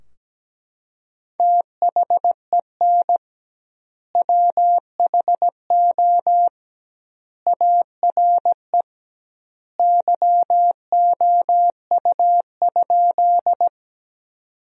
Morse code, heard as a keyed tone, about 17 wpm, 700 Hz. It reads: THEN WHO ARE YOU?
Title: 2x Word-Spacing - Sentences from Top 100 Words - 17wpm